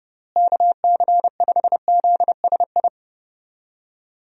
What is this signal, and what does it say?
Morse code, heard as a keyed tone, about 30 wpm, 700 Hz. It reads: KC5ZSI
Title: USA Call Signs - 6 Characters - Repeated 2x - 30wpm